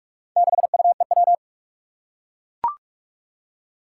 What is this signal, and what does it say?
Morse code, heard as a keyed tone, about 44 wpm, 700 Hz. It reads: 6UEW
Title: Four Challenging Letter-Number Combinations 45wpm